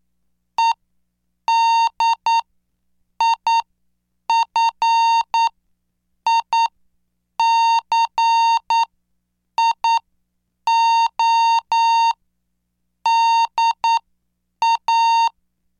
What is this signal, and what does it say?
Morse code, heard as a keyed tone, about 9 wpm, spaced as for 7 wpm, 920 Hz. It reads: EDIFICIO DA